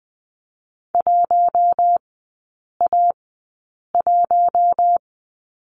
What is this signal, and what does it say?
Morse code, heard as a keyed tone, about 20 wpm, spaced as for 8 wpm, 700 Hz. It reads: 1A1
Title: 041 Lesson 8e - Number 1 - Three Characters 20wpm